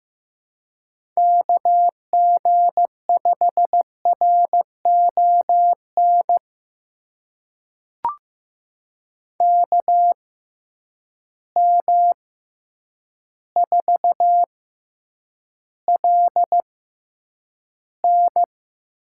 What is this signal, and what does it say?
Morse code, heard as a keyed tone, about 15 wpm, spaced as for 8 wpm, 700 Hz. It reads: KG5RON KM4LN
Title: USA Call Signs - 6 Characters - 6x Character Spacing - 15wpm